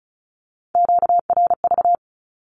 Morse code, heard as a keyed tone, about 34 wpm, 700 Hz. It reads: QRV
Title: Q-Codes Repeated 3x 35wpm